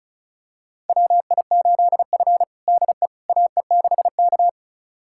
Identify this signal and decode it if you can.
Morse code, heard as a keyed tone, about 36 wpm, 700 Hz. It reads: WI8F DE AE6K
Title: USA Callsign DE Callsign - 4 Characters - 35wpm